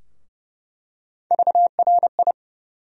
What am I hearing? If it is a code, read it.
Morse code, heard as a keyed tone, about 30 wpm, 700 Hz. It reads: VRI
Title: USA Call Sign Postfixes - 3 Letters 30wpm